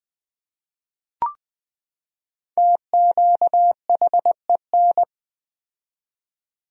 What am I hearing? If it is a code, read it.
Morse code, heard as a keyed tone, about 20 wpm, 700 Hz. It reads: TQHEN
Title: 4x Word-Spacing - Sentences from Top 100 Words - 20wpm